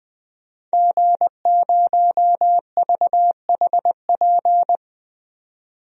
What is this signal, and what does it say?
Morse code, heard as a keyed tone, about 20 wpm, 700 Hz. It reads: G0VHP
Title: DXCC-100 Call Signs - 5 Characters - 20wpm